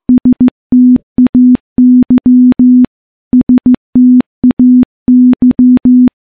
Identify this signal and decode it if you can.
Morse code, heard as a keyed tone, about 15 wpm, 250 Hz. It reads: STAY STAY
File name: morse-training-15WPM